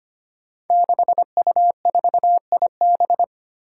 Morse code, heard as a keyed tone, about 25 wpm, 700 Hz. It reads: 6U4IB
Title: Five Challenging Letter-Number Combinations 25wpm